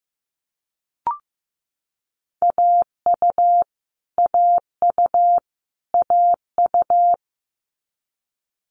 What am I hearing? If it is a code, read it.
Morse code, heard as a keyed tone, about 15 wpm, 700 Hz. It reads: AU AU AU
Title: Multi-letter Phonemes Repeated 3x 15wpm